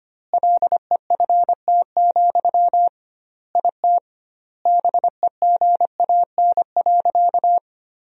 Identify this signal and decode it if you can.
Morse code, heard as a keyed tone, about 25 wpm, 700 Hz. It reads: LEFT, IT BEGAN.
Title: 2x Word-Spacing - Sentences from Top 400 Words - 25wpm